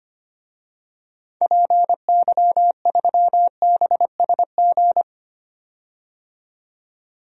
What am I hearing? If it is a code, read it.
Morse code, heard as a keyed tone, about 25 wpm, 700 Hz. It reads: PY3BSG
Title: DXCC-100 Call Signs - 6 Characters - 25wpm